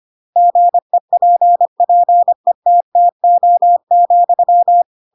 Morse code, heard as a keyed tone, about 25 wpm, 700 Hz. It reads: GEPPETTO,